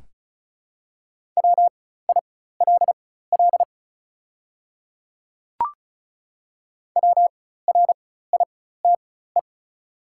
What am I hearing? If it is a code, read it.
Morse code, heard as a keyed tone, about 34 wpm, spaced as for 16 wpm, 700 Hz. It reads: WILL WRITE
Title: 4x Character Spacing - Top 100 Words - 35wpm